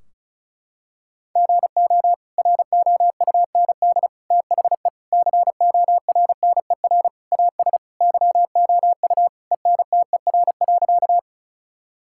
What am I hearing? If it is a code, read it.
Morse code, heard as a keyed tone, about 34 wpm, 700 Hz. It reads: GO ROUND THE CORNER AS YOU ENTER.
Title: Sentences from Top 1600 Words 35wpm